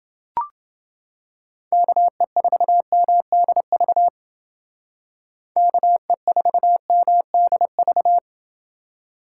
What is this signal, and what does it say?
Morse code, heard as a keyed tone, about 28 wpm, 700 Hz. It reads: KE4MDV KE4MDV